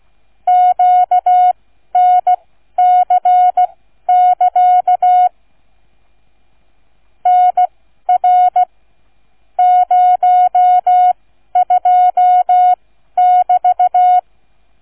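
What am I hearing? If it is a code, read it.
Morse code, heard as a keyed tone, about 16 wpm, spaced as for 12 wpm, 730 Hz. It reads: QNC<KA> NR 02<BT>
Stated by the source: Morse code receive practice, flight number, from ZL3TK.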